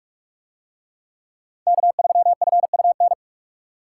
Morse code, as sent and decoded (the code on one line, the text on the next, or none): -.- ...-- ..-. ..- -.
K3FUN